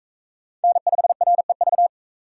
Morse code, heard as a keyed tone, about 42 wpm, 700 Hz. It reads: N5REV